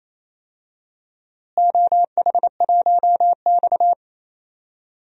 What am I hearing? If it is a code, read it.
Morse code, heard as a keyed tone, about 28 wpm, 700 Hz. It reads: OH1X
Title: DXCC-100 Call Signs - 4 Characters - Repeated 3x - 28wpm